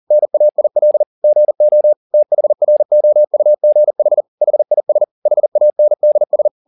Morse code, heard as a keyed tone, about 40 wpm, 600 Hz. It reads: NAIL GO THROUGH HIS HANDS
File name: the old man and the sea 2 40wpm